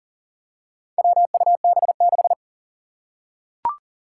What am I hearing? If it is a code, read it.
Morse code, heard as a keyed tone, about 40 wpm, 700 Hz. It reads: WUB6
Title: Four Challenging Letter-Number Combinations 40wpm